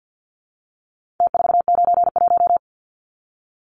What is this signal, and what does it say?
Morse code, heard as a keyed tone, about 48 wpm, 700 Hz. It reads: T491
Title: Numbers - Contest Style - 4-Digit 50wpm